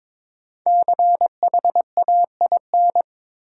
Morse code, heard as a keyed tone, about 22 wpm, 700 Hz. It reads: CHAIN
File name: 5 Letter Words - First 3 Letters Given - 22wpm